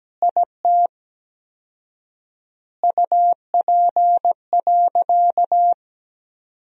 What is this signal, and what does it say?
Morse code, heard as a keyed tone, about 17 wpm, 700 Hz. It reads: IT UP.
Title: 4x Word-Spacing - Sentences from Top 100 Words - 17wpm